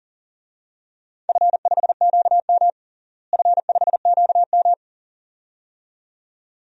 Morse code, heard as a keyed tone, about 40 wpm, 700 Hz. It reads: F5QM F5QM
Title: DXCC-100 Call Signs - 4 Characters - Repeated 2x - 40wpm